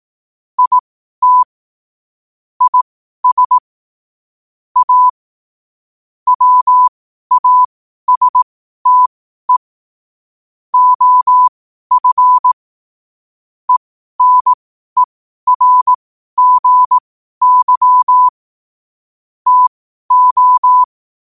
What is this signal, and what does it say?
Morse code, heard as a keyed tone, about 18 wpm, spaced as for 12 wpm, 1000 Hz. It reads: IT IS A WASTE OF ENERGY TO
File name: qotdfiles / 12wpm